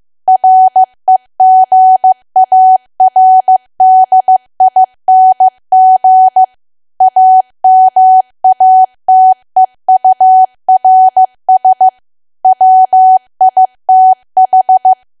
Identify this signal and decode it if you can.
Morse code, heard as a keyed tone, about 15 wpm, 750 Hz. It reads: REGARDING AMATEURS WITH
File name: cwa / W1AW-Level2 / 15WPM